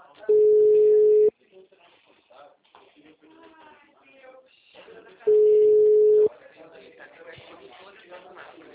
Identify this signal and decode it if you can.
Ringback tone (Europe, 425 Hz), with speech.